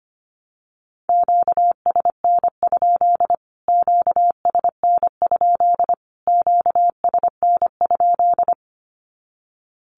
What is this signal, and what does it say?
Morse code, heard as a keyed tone, about 25 wpm, 700 Hz. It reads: QSN? QSN? QSN?
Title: Q-Codes Repeated 3x 25wpm